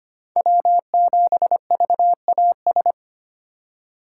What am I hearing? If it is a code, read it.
Morse code, heard as a keyed tone, about 25 wpm, 700 Hz. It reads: W7VAS
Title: K1USN SST QSO Practice - 25wpm